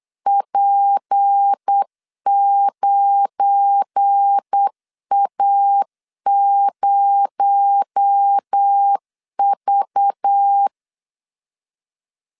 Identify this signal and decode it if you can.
Morse code, heard as a keyed tone, about 8 wpm, 800 Hz. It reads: P9A0V
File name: CW / DET / 1 9 1500C S C 8wpm 5653